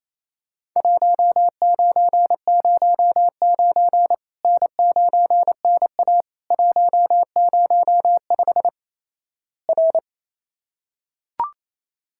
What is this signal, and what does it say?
Morse code, heard as a keyed tone, about 28 wpm, 700 Hz. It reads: 1909 N9NA 105 R
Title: Worked All Europe DX Contest - QTC Exchange - 28wpm